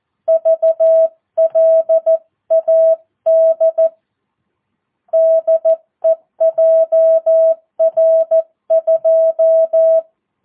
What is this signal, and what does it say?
Morse code, heard as a keyed tone, about 14 wpm, 650 Hz. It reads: VLAD DEJR2
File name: jh8cbh / 2022cw